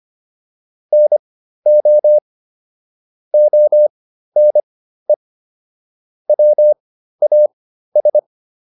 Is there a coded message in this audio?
Yes: Morse code from 0.9 to 8.2 s, heard as a keyed tone, about 25 wpm, spaced as for 13 wpm, 600 Hz. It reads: NO ONE WAS